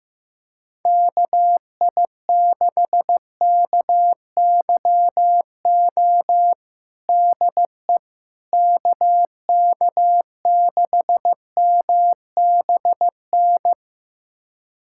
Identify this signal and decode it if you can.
Morse code, heard as a keyed tone, about 15 wpm, 700 Hz. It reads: KI6KYO DE KK6MBN
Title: USA Callsign DE Callsign - 6 Characters - 15wpm